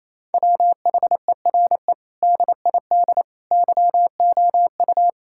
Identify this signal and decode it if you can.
Morse code, heard as a keyed tone, about 28 wpm, 690 Hz. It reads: WHERE DID YOU
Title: Sentences from Top 1000 Words - Review Part 6 - 28wpm